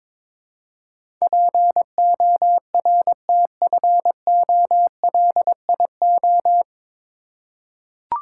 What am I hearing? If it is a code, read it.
Morse code, heard as a keyed tone, about 22 wpm, 700 Hz. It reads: PORTFOLIO
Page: Morse Code Ninja practice 22wpm